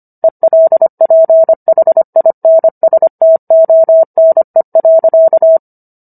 Morse code, heard as a keyed tone, about 25 wpm, 650 Hz. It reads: ELPHINSTONE.